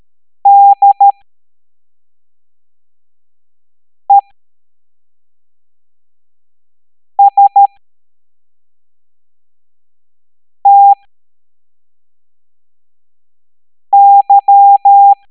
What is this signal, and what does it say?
Morse code, heard as a keyed tone, about 13 wpm, spaced as for 3 wpm, 800 Hz. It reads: DESTY
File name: hamdata / L05-EX01-SVY